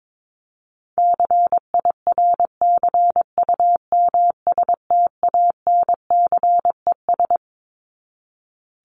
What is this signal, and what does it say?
Morse code, heard as a keyed tone, about 22 wpm, 700 Hz. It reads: CIRCUMSTANCES
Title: ES:S Suffix - Top 100 Words with Base Words 22wpm